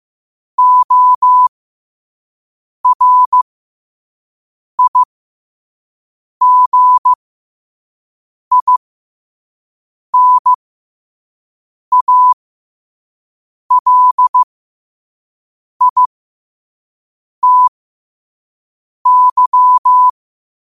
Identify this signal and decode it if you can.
Morse code, heard as a keyed tone, about 15 wpm, spaced as for 5 wpm, 1000 Hz. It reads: ORIGINALITY
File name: qotdfiles / 5wpm